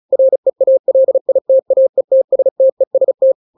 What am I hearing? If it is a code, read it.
Morse code, heard as a keyed tone, about 36 wpm, 530 Hz. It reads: REALITAETSTEST
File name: daily / 35wpm